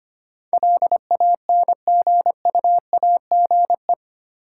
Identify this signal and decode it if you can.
Morse code, heard as a keyed tone, about 28 wpm, 700 Hz. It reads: LANGUAGE
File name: Easy Words - Speed-Racing with Hints - 25wpm